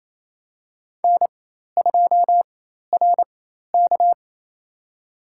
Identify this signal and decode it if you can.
Morse code, heard as a keyed tone, about 28 wpm, spaced as for 13 wpm, 700 Hz. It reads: N2RK
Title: USA Call Signs - 4 Characters - 4x Character Spacing - 28wpm